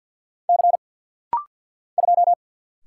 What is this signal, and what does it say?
Morse code, heard as a keyed tone, about 50 wpm, 700 Hz. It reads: C 2
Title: Single Letter-Number - Rapid-Fire 50wpm